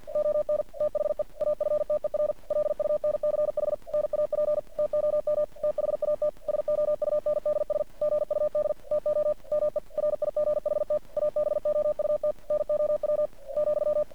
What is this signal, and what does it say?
Morse code, heard as a keyed tone, about 48 wpm, 600 Hz. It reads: ON THE AFTER LUNCH NAP TOM THAT SOUNDS GUD TO ME RIGHT ABOUT NOW ,
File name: archived / CRSnet